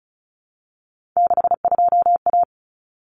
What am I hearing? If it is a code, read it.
Morse code, heard as a keyed tone, about 34 wpm, 700 Hz. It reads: 62A
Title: Common Cut Numbers - 3-Digit 35wpm